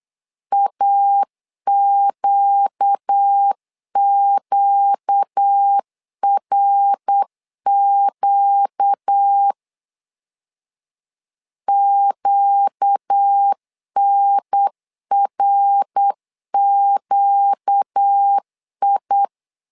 Morse code, heard as a keyed tone, about 8 wpm, 800 Hz. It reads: AQQRQ QNRQI
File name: LEZIONI / 4 Q 8wpm 2732